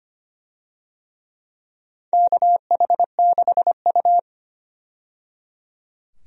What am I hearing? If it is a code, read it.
Morse code, heard as a keyed tone, about 25 wpm, 700 Hz. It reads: KH6U
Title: DXCC-100 Call Signs - 4 Characters - 25wpm